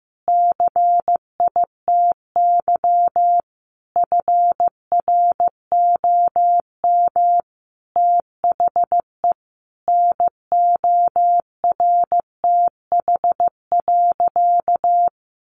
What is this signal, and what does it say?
Morse code, heard as a keyed tone, about 15 wpm, 700 Hz. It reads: CITY FROM THE NORTH.
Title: Sentences from Top 1200 Words 15wpm